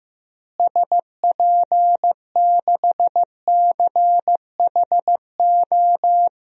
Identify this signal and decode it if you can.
Morse code, heard as a keyed tone, about 15 wpm, 700 Hz. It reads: SP6CHO
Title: DXCC-100 Call Signs - 6 Characters - Repeated 3x - 15wpm